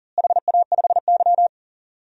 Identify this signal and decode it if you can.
Morse code, heard as a keyed tone, about 37 wpm, 700 Hz. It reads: HA5Y
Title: DXCC-100 Call Signs - 4 Characters - Repeated 3x - 40wpm